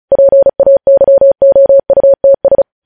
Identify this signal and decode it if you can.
Morse code, heard as a keyed tone, about 36 wpm, 550 Hz. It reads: PAYOUTS
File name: podcast / News headlines-1x-35x35